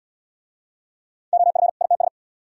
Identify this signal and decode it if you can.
Morse code, heard as a keyed tone, about 76 wpm, 700 Hz. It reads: OF ITS